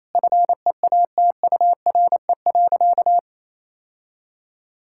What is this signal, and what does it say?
Morse code, heard as a keyed tone, about 28 wpm, 700 Hz. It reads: FEATURE.
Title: Sentences from Top 1000 Words - Review Part 5 - 28wpm